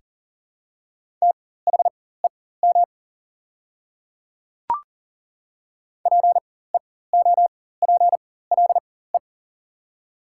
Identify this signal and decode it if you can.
Morse code, heard as a keyed tone, about 40 wpm, spaced as for 19 wpm, 700 Hz. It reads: THEM PEOPLE
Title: 4x Character Spacing - Top 200 Words - 40wpm